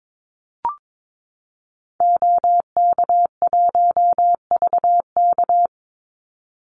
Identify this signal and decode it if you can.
Morse code, heard as a keyed tone, about 22 wpm, 700 Hz. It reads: OK1VK
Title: Call Signs - 22wpm